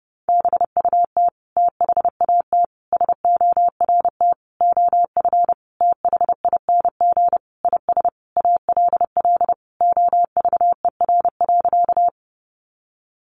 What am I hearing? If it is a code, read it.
Morse code, heard as a keyed tone, about 30 wpm, 700 Hz. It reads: BUT THAT SORT OF THING IS ALL OVER.
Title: Sentences from Top 1300 Words 30wpm